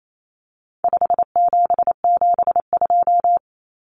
Morse code, heard as a keyed tone, about 28 wpm, 700 Hz. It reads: 5772
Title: Common Cut Numbers - 4-Digit 28wpm